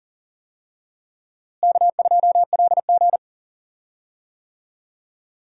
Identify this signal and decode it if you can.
Morse code, heard as a keyed tone, about 37 wpm, 700 Hz. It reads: K2LG
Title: USA Call Signs - 4 Characters - Repeated 2x - 40wpm